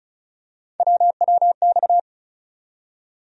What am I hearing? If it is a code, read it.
Morse code, heard as a keyed tone, about 34 wpm, 700 Hz. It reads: WWX
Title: USA Call Sign Postfixes - 3 Letters 35wpm